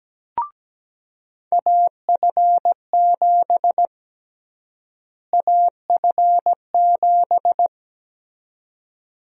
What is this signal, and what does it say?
Morse code, heard as a keyed tone, about 17 wpm, 700 Hz. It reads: AF7 AF7
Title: USA Call Sign Prefixes - Repeated 2x - 17wpm